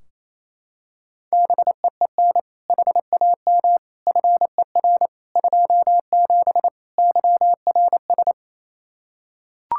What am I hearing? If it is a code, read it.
Morse code, heard as a keyed tone, about 28 wpm, 700 Hz. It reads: BEEN HAM FER 27 YRS
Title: Practice QSO - Exchange 3 - More Info - 28wpm